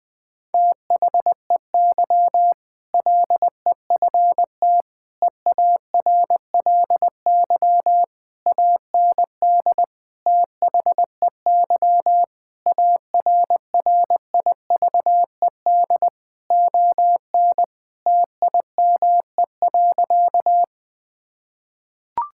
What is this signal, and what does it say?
Morse code, heard as a keyed tone, about 20 wpm, 700 Hz. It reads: THEY LEFT EARLY AND THEY ARRIVED ON TIME.
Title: Sentences with 2-Word Phrases - 20wpm